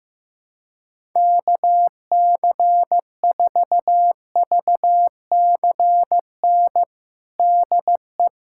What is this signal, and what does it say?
Morse code, heard as a keyed tone, about 15 wpm, 700 Hz. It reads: KC4VCN DE
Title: USA Callsign DE Callsign - 6 Characters - 15wpm